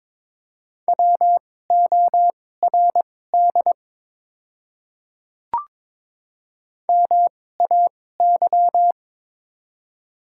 Morse code, heard as a keyed tone, about 22 wpm, spaced as for 16 wpm, 700 Hz. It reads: WORD MAY
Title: 2x Character Spacing - Top 200 Words - 22wpm